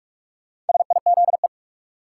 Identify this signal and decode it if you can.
Morse code, heard as a keyed tone, about 46 wpm, 700 Hz. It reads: SIZE
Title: Sets of 3 Words - New Word Drill - 45wpm